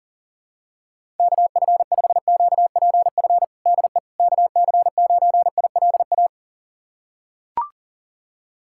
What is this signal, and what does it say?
Morse code, heard as a keyed tone, about 37 wpm, 700 Hz. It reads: KF5QPF DE KC9ILA
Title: USA Callsign DE Callsign 40wpm